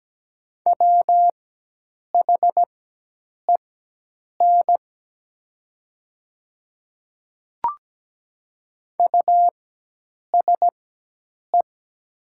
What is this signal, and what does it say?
Morse code, heard as a keyed tone, about 17 wpm, spaced as for 8 wpm, 700 Hz. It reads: WHEN USE
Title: 4x Character Spacing - Top 100 Words - 17wpm